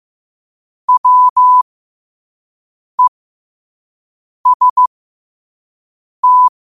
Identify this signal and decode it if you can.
Morse code, heard as a keyed tone, about 15 wpm, spaced as for 5 wpm, 1000 Hz. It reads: WEST
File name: qotdfiles / 5wpm